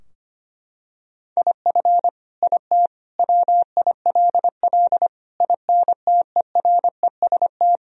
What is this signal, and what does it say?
Morse code, heard as a keyed tone, about 25 wpm, 700 Hz. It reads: IF IT WILL INTEREST
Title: Sentences from Top 1000 Words - Review Part 2 - 25wpm